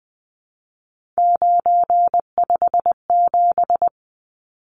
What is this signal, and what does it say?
Morse code, heard as a keyed tone, about 20 wpm, 700 Hz. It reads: 957
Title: Numbers - 3-Digit 20wpm